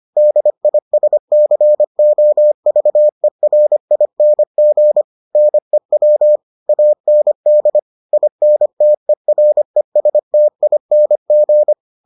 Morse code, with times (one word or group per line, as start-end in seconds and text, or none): 0.2-5.0 DISCOVERING
5.3-6.4 NEW
6.7-7.8 AND
8.1-11.7 INTERESTING